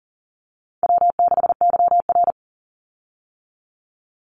Morse code, heard as a keyed tone, about 40 wpm, 700 Hz. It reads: W6YR
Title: Call Signs Repeated 3x - 40wpm